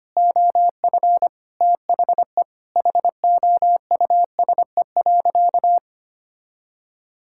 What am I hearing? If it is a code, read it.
Morse code, heard as a keyed tone, about 25 wpm, 700 Hz. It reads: OF THE HOUSE.
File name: practice / 25wpm / Sentences with 4-Word Phrases - 25wpm